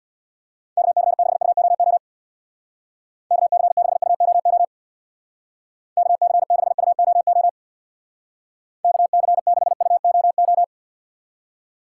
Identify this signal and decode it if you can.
Morse code, heard as a keyed tone, about 64 wpm, 700 Hz. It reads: KC6UOO KC6UOO KC6UOO KC6UOO